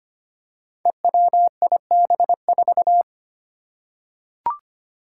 Morse code, heard as a keyed tone, about 25 wpm, 700 Hz. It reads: EWIB4
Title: Five Challenging Letter-Number Combinations 25wpm